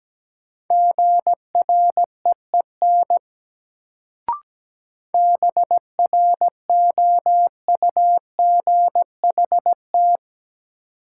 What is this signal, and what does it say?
Morse code, heard as a keyed tone, about 17 wpm, 700 Hz. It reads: GREEN BROUGHT
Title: Sentences from Top 500 Words - New Word Drill - 17wpm